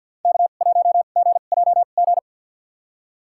Morse code, heard as a keyed tone, about 52 wpm, spaced as for 36 wpm, 700 Hz. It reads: K1GJG
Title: USA Call Signs - 2x Character Spacing - 50wpm